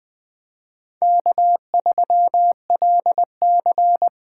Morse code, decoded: K3LC

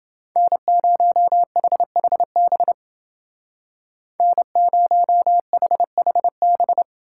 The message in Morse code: N0HHB N0HHB

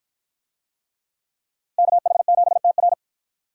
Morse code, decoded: KH7TR

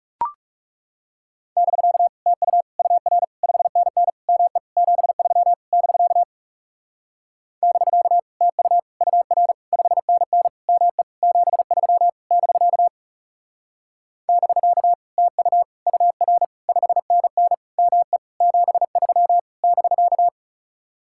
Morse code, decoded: <BK> TU UR 5NN ME 73 <BK> <BK> TU UR 5NN ME 73 <BK> <BK> TU UR 5NN ME 73 <BK>

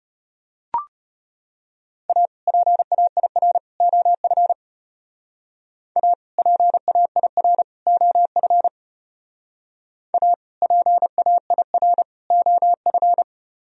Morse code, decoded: A PAIR OF A PAIR OF A PAIR OF

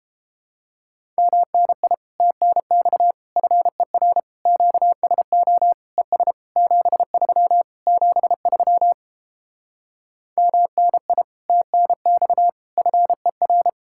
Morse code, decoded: MNI TNX FER QSO ES 73 73 MNI TNX FER

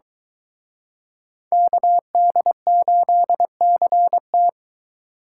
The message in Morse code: KD8CT

